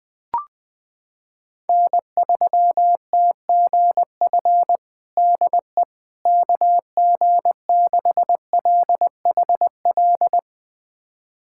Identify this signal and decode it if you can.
Morse code, heard as a keyed tone, about 20 wpm, 700 Hz. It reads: N3TGF DE KG6LHL
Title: USA Callsign DE Callsign 20wpm